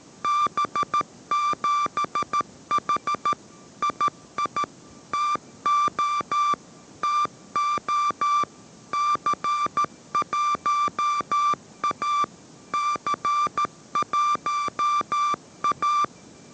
Morse code, heard as a keyed tone, about 13 wpm, 1180 Hz. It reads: B7H II TO TO C1A C1A